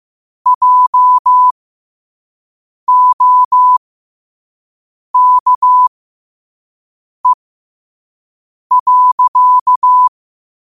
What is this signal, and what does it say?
Morse code, heard as a keyed tone, about 15 wpm, spaced as for 5 wpm, 1000 Hz. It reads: JOKE.
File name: qotdfiles / 5wpm